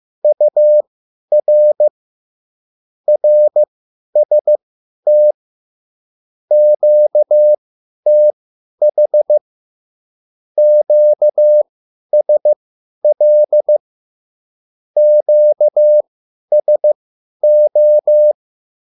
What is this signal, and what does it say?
Morse code, heard as a keyed tone, about 15 wpm, spaced as for 10 wpm, 600 Hz. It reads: UR RST QTH QSL QSO
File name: QSL / CW lesson